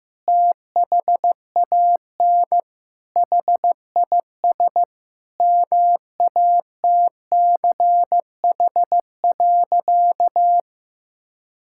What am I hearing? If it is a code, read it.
Morse code, heard as a keyed tone, about 15 wpm, 700 Hz. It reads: THAN HIS MATCH.
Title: Sentences from Top 1600 Words 15wpm